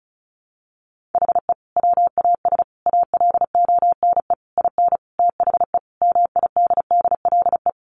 Morse code, heard as a keyed tone, about 34 wpm, 700 Hz. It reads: HE WAS ALONE IN THE MIDDLE
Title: Sentences from Top 1000 Words 35wpm